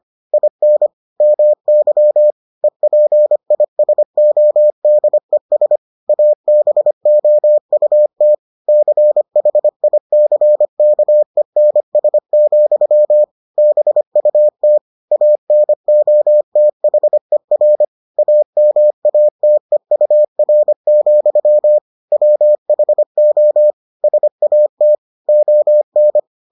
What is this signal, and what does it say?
Morse code, heard as a keyed tone, about 25 wpm, 600 Hz. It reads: IN MY EPISODES ABOUT CHICKENS, BUT ANOTHER AMATEUR, WHO SAT ON